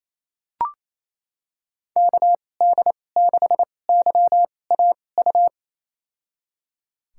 Morse code, heard as a keyed tone, about 28 wpm, spaced as for 20 wpm, 700 Hz. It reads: KD6YAU
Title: USA Call Signs - 2x Character Spacing - 28wpm